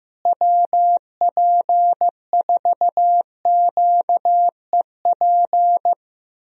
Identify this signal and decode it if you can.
Morse code, heard as a keyed tone, about 15 wpm, 700 Hz. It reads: WP4QEP
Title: USA Call Signs - 6 Characters - Repeated 3x - 15wpm